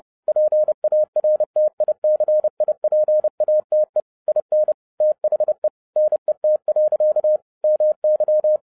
Morse code, heard as a keyed tone, about 30 wpm, 610 Hz. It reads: PARTICIPATE IN THE NET. MY